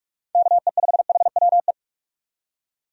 Morse code, heard as a keyed tone, about 46 wpm, 700 Hz. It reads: KE5HWE
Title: USA Call Signs - 6 Characters - 45wpm